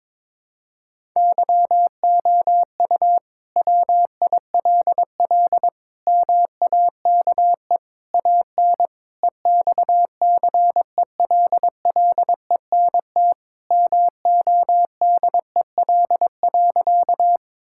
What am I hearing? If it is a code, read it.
Morse code, heard as a keyed tone, about 22 wpm, 700 Hz. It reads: YOU WILL MAKE AN EXCELLENT MODEL.
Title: Sentences from Top 1000 Words - Review Part 5 - 22wpm